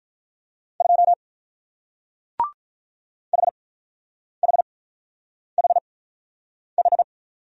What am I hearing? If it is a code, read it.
Morse code, heard as a keyed tone, about 44 wpm, spaced as for 9 wpm, 700 Hz. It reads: 2 HHHH